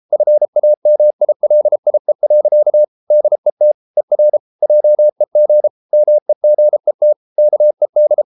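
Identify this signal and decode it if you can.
Morse code, heard as a keyed tone, about 33 wpm, 600 Hz. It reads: FAMILIE. DET ER JEG MEGET KED
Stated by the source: Morse code at 33 wpm, QRQ.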